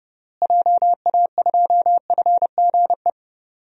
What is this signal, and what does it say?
Morse code, heard as a keyed tone, about 30 wpm, 700 Hz. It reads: JA2FGE
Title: DXCC-100 Call Signs - 6 Characters - Repeated 2x - 30wpm